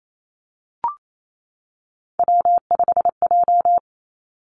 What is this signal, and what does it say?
Morse code, heard as a keyed tone, about 28 wpm, 700 Hz. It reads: W5J
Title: Three Challenging Letter-Number Combinations 28wpm